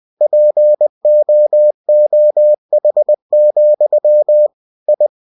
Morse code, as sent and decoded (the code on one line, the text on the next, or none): .--. --- --- .... --..-- / ..
POOH, I